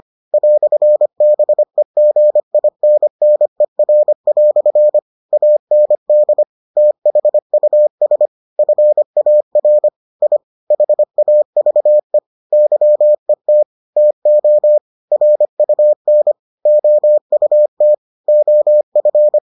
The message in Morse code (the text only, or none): "BEGINNER" AND THUS FAR I HAVE YET TO RUN OUT OF